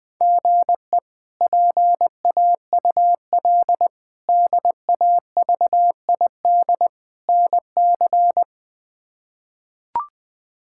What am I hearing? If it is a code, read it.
Morse code, heard as a keyed tone, about 21 wpm, 700 Hz. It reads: GE PAUL DAVID NC
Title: K1USN SST QSO Practice - 20wpm